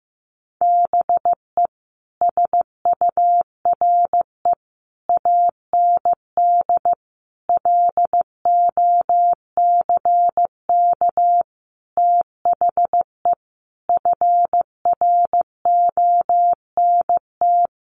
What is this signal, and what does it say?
Morse code, heard as a keyed tone, about 15 wpm, 700 Hz. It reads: BE SURE AND LOCK THE FRONT
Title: Sentences from Top 1300 Words 15wpm